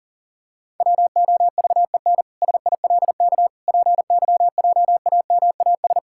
Morse code, heard as a keyed tone, about 37 wpm, 700 Hz. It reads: WOVEN SILK PYJAMAS
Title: Pangram Challenge v2 - 40wpm